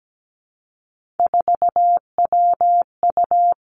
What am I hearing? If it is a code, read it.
Morse code, heard as a keyed tone, about 17 wpm, 700 Hz. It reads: 4WU
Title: Three Challenging Letter-Number Combinations 17wpm